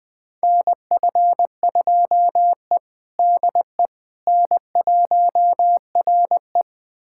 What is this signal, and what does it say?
Morse code, heard as a keyed tone, about 20 wpm, 700 Hz. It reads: NF2E DE N1RE
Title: USA Callsign DE Callsign - 4 Characters - 20wpm